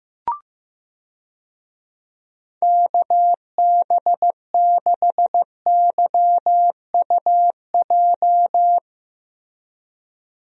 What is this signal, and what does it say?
Morse code, heard as a keyed tone, about 15 wpm, 700 Hz. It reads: KB6YUJ
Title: USA Call Signs - 6 Characters - Repeated 2x - 15wpm